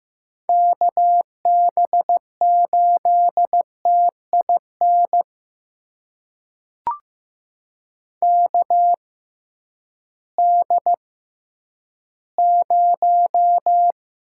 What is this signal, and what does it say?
Morse code, heard as a keyed tone, about 15 wpm, spaced as for 10 wpm, 700 Hz. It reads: KB8TIN KD0
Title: USA Call Signs - 6 Characters - 6x Character Spacing - 15wpm